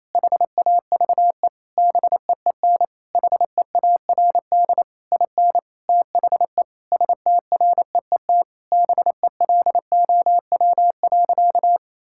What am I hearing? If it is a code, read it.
Morse code, heard as a keyed tone, about 27 wpm, 700 Hz. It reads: HAVE BEEN HEARD IN THE STREET BELOW.
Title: Sentences from Top 1000 Words - Review Part 5 - 28wpm